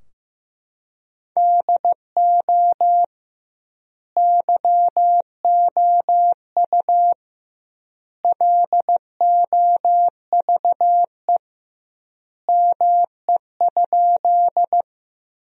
Morse code, heard as a keyed tone, about 15 wpm, 700 Hz. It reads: DO YOU LOVE ME?
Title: 2x Word-Spacing - Sentences from Top 400 Words - 15wpm